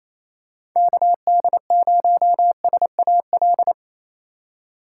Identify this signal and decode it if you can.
Morse code, heard as a keyed tone, about 28 wpm, 700 Hz. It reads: KD0SAL
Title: USA Call Signs - 6 Characters - 4x Character Spacing - 28wpm